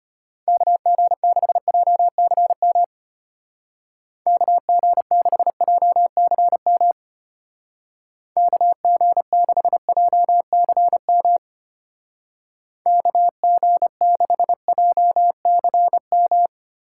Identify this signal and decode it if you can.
Morse code, heard as a keyed tone, about 30 wpm, 700 Hz. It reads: KG6JCM KG6JCM KG6JCM KG6JCM
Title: Speed Racing - USA 2x3 Call Signs 25wpm